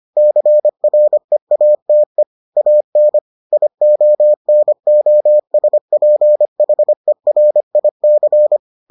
Morse code, heard as a keyed tone, about 25 wpm, 600 Hz. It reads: CREATE AN IONOSPHERIC